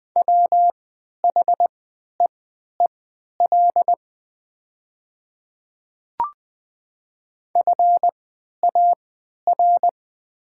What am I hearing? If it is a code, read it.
Morse code, heard as a keyed tone, about 20 wpm, spaced as for 11 wpm, 700 Hz. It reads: WHEEL FAR
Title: 3x Character Spacing - Top 500 Words - 20wpm